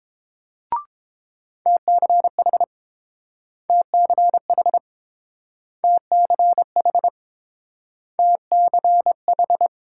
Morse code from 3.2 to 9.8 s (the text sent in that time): TCH TCH TCH